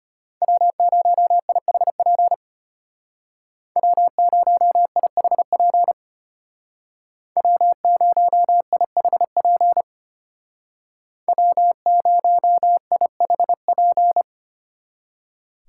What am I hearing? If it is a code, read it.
Morse code, heard as a keyed tone, about 31 wpm, 700 Hz. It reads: W0IHP W0IHP W0IHP W0IHP
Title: Speed Racing - USA 1x3 Call Signs 25wpm